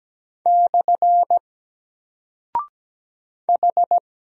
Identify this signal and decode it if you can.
Morse code, heard as a keyed tone, about 17 wpm, 700 Hz. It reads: / H
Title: Single Character (Letter:Number:Punctuation:Prosign) 17wpm